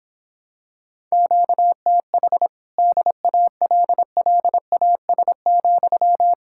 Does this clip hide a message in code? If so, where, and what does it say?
Yes: Morse code from 1.1 to 6.3 s, heard as a keyed tone, about 26 wpm, 700 Hz. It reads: QTH DALLAS,